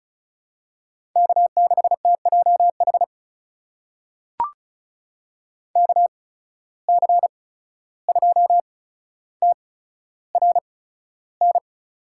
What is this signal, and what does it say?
Morse code, heard as a keyed tone, about 36 wpm, spaced as for 14 wpm, 700 Hz. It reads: K6TJH KC2TRN
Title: USA Call Signs - 8x Character Spacing - 35wpm